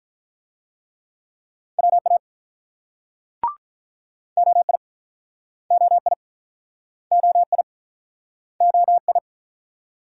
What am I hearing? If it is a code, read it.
Morse code, heard as a keyed tone, about 46 wpm, 700 Hz. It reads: WA OI OI OI OI